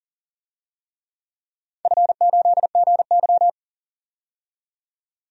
Morse code, heard as a keyed tone, about 40 wpm, 700 Hz. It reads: F8GY
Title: DXCC-100 Call Signs - 4 Characters - 40wpm